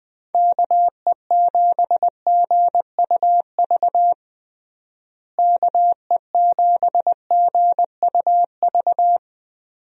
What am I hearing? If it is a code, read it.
Morse code, heard as a keyed tone, about 21 wpm, 700 Hz. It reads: KE7GUV KE7GUV
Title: USA Call Signs - 6 Characters - Repeated 2x - 20wpm